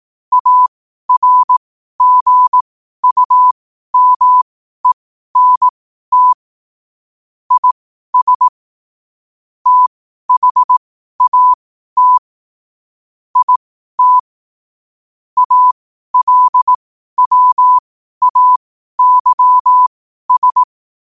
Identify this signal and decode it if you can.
Morse code, heard as a keyed tone, about 18 wpm, spaced as for 12 wpm, 1000 Hz. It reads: ARGUMENT IS THAT IT ALWAYS
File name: qotdfiles / 12wpm